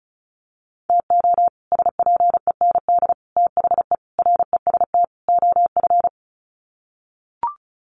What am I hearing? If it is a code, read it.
Morse code, heard as a keyed tone, about 34 wpm, 700 Hz. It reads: TO SPEND THE REST OF
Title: Sets of 5 Words Encore2 35wpm